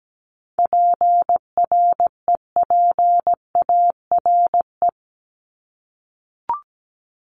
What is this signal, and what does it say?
Morse code, heard as a keyed tone, about 17 wpm, 700 Hz. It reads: PREPARE
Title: ED Suffix - Top 100 Words with Base Words 17wpm